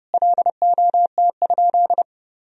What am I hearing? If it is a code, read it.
Morse code, heard as a keyed tone, about 30 wpm, 700 Hz. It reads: LOT?
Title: Sentences from Top 1000 Words - Review Part 4 - 30wpm